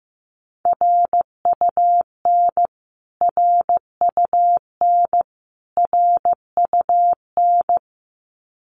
Morse code, heard as a keyed tone, about 15 wpm, 700 Hz. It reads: RUN RUN RUN
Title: Top 300 Words Repeated 15wpm